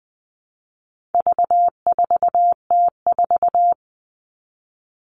Morse code, heard as a keyed tone, about 20 wpm, 700 Hz. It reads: V4T4
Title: All Cut Numbers - 4-Digit 20wpm